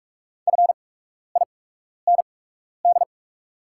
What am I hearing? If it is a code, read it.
Morse code, heard as a keyed tone, about 44 wpm, spaced as for 12 wpm, 700 Hz. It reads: FIND
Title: 8x Character Spacing - Top 100 Words - 45wpm